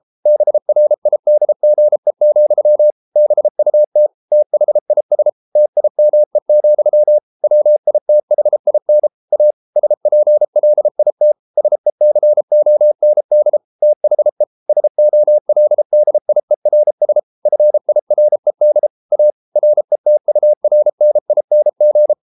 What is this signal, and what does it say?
Morse code, heard as a keyed tone, about 33 wpm, 600 Hz. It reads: BRIDGE, BUT THIS TIME, WITHIN A SPLIT SECOND THE SOLDIERS FIRED A RETURNING